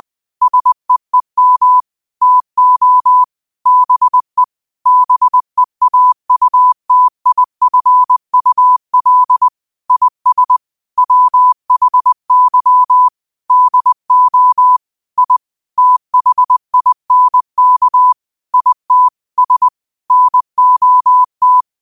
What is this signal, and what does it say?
Morse code, heard as a keyed tone, about 20 wpm, 1000 Hz. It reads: SEEM TO BE BEAUTIFUL IS WHY DO I THINK IT S NOT